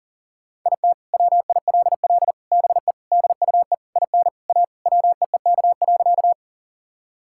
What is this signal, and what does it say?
Morse code, heard as a keyed tone, about 40 wpm, 700 Hz. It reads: IT WILL BE DUE IN A WEEK.
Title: Sentences from Top 1000 Words - Review Part 6 - 40wpm